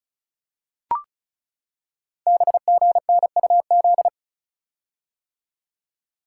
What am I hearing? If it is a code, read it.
Morse code, heard as a keyed tone, about 36 wpm, 700 Hz. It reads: BGNUZ
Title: Code Groups - 5 Letters - 35wpm